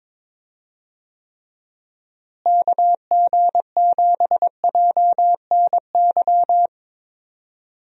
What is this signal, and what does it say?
Morse code, heard as a keyed tone, about 22 wpm, 700 Hz. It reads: KG7JNY
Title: USA Call Signs - 6 Characters - Repeated 2x - 22wpm